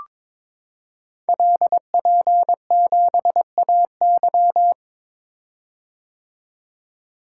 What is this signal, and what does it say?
Morse code, heard as a keyed tone, about 22 wpm, 700 Hz. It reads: LP7AY
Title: Code Groups - 5 Letter:Numbers - 22wpm